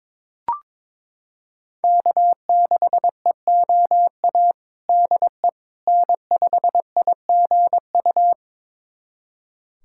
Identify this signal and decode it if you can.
Morse code, heard as a keyed tone, about 22 wpm, 700 Hz. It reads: K6EOA DE N5IGU
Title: USA Callsign DE Callsign - 5 Characters - 22wpm